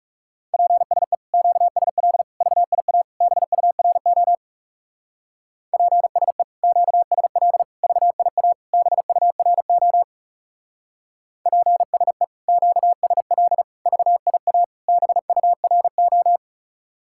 Morse code, heard as a keyed tone, about 39 wpm, 700 Hz. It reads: PSE QSL VIA BURO PSE QSL VIA BURO PSE QSL VIA BURO